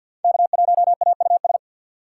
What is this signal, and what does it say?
Morse code, heard as a keyed tone, about 64 wpm, 700 Hz. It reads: K1AUS